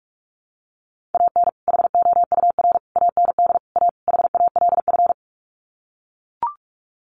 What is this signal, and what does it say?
Morse code, heard as a keyed tone, about 44 wpm, 700 Hz. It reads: AN HOUR AND A HALF
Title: Sets of 5 Words 45wpm